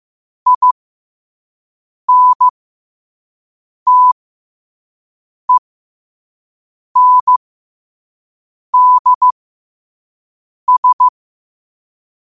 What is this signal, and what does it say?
Morse code, heard as a keyed tone, about 15 wpm, spaced as for 5 wpm, 1000 Hz. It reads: INTENDS
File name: qotdfiles / 5wpm